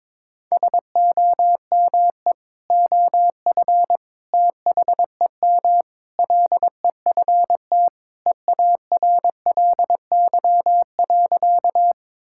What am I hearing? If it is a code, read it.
Morse code, heard as a keyed tone, about 22 wpm, 700 Hz. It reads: SOME OF THEM LEFT EARLY.
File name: practice / 22wpm / Sentences with 3-Word Phrases - 22wpm